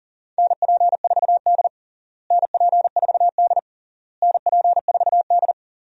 Morse code, heard as a keyed tone, about 37 wpm, 700 Hz. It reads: NP4D NP4D NP4D
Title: USA Call Signs - 4 Characters - Repeated 3x - 40wpm